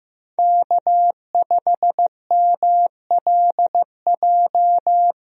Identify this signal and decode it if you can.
Morse code, heard as a keyed tone, about 15 wpm, 700 Hz. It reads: K5MLJ
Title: USA Call Signs - 5 Characters - Repeated 2x - 15wpm